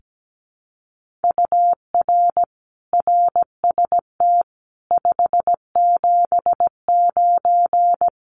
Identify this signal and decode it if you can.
Morse code, heard as a keyed tone, about 17 wpm, 700 Hz. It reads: UR RST 579
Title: Common QSO Phrases Encore 17wpm